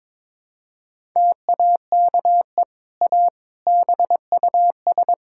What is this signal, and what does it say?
Morse code, heard as a keyed tone, about 22 wpm, 700 Hz. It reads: TAKE A BUS
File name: practice / 22wpm / Easy Sentences with Hints - 22wpm